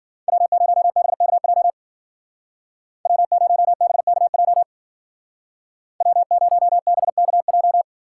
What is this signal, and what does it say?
Morse code, heard as a keyed tone, about 49 wpm, 700 Hz. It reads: W0BKJ W0BKJ W0BKJ
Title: Speed Racing - USA 1x3 Call Signs 40wpm